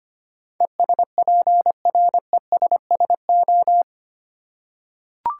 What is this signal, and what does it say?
Morse code, heard as a keyed tone, about 25 wpm, 700 Hz. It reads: ESPRESSO
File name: practice / 25wpm / Easy Words with Hints - 25wpm